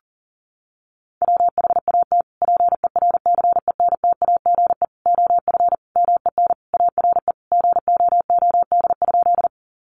Morse code, heard as a keyed tone, about 40 wpm, 700 Hz. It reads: WHAT PERCENTAGE OF MEN ARE GOOD?
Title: Sentences from Top 1300 Words 40wpm